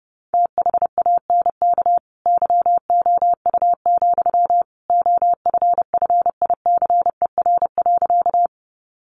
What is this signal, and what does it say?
Morse code, heard as a keyed tone, about 30 wpm, 700 Hz. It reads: THANK YOU, OFFICER.
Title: Sentences from Top 1200 Words 30wpm